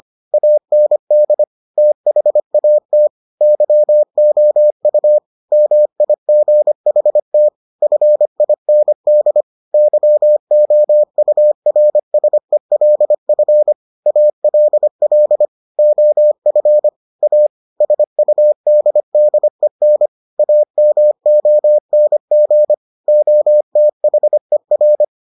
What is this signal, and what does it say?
Morse code, heard as a keyed tone, about 25 wpm, 600 Hz. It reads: AND THAT YOU MIGHT FIND YOURSELF ALL OF A SUDDEN AMONG OTHER